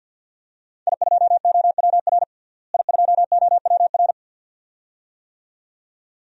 Morse code, heard as a keyed tone, about 51 wpm, 700 Hz. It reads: I2OWR I2OWR
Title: DXCC-100 Call Signs - 5 Characters - Repeated 2x - 50wpm